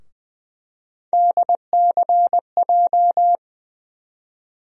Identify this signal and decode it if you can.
Morse code, heard as a keyed tone, about 20 wpm, 700 Hz. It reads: DCJ
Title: USA Call Sign Postfixes - 3 Letters 20wpm